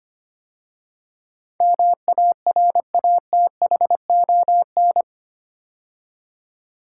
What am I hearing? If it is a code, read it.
Morse code, heard as a keyed tone, about 25 wpm, 700 Hz. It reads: MARATHON
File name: 8 Letter Words - First 3 Letters Given - 25wpm